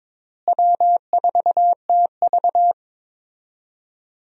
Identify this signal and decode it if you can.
Morse code, heard as a keyed tone, about 22 wpm, 700 Hz. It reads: W4TV
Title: USA Call Signs - 4 Characters - Repeated 2x - 22wpm